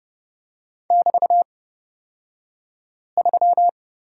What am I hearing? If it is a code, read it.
Morse code, heard as a keyed tone, about 29 wpm, spaced as for 5 wpm, 700 Hz. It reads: <BT>3
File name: Lesson 36d - BT - Two Characters 30wpm